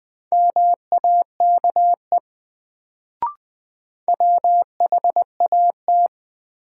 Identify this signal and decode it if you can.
Morse code, heard as a keyed tone, about 20 wpm, 700 Hz. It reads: MAKE WHAT
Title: Sets of 2 Words - New Word Drill - 20wpm